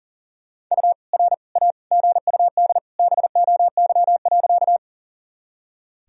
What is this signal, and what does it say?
Morse code, heard as a keyed tone, about 40 wpm, 700 Hz. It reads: U R A GUD BOY.
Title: Sentences from Top 200 Words and Abbreviations 40wpm